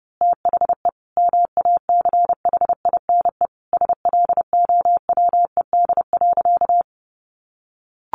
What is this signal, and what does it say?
Morse code, heard as a keyed tone, about 30 wpm, 700 Hz. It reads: THE MACHINE SLOWED.
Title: A Sound Of Thunder 30wpm